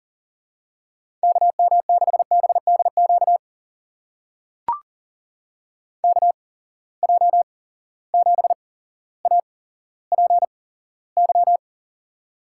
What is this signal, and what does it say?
Morse code, heard as a keyed tone, about 40 wpm, spaced as for 17 wpm, 700 Hz. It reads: KM6BDQ KJ7APY
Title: USA Call Signs - 6 Characters - 8x Character Spacing - 40wpm